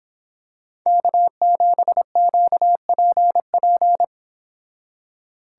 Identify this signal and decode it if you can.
Morse code, heard as a keyed tone, about 26 wpm, 700 Hz. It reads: K7QPP